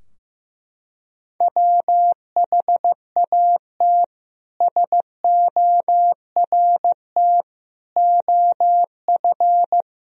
Morse code, heard as a keyed tone, about 15 wpm, 700 Hz. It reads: WHAT SORT OF
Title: Sentences from Top 1000 Words - Review Part 3 - 15wpm